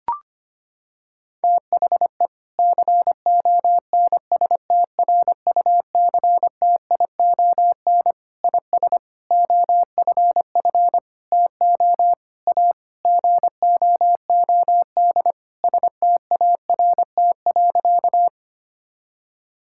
Morse code, heard as a keyed tone, about 25 wpm, 700 Hz. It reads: THE CONSTRUCTION IS OFF TO A GOOD START.
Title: Sentences from Top 1600 Words 25wpm